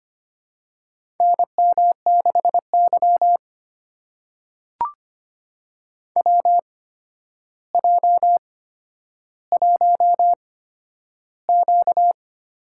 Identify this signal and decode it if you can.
Morse code, heard as a keyed tone, about 25 wpm, spaced as for 11 wpm, 700 Hz. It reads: NM6Y WJ1Q
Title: USA Call Signs - 4 Characters - 8x Character Spacing - 25wpm